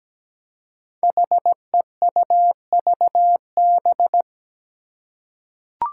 Morse code, heard as a keyed tone, about 17 wpm, 700 Hz. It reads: HEUVB
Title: Five Challenging Letter-Number Combinations 17wpm